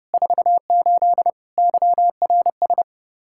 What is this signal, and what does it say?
Morse code, heard as a keyed tone, about 30 wpm, 700 Hz. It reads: 48 YRS
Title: K1USN SST QSO Practice 30wpm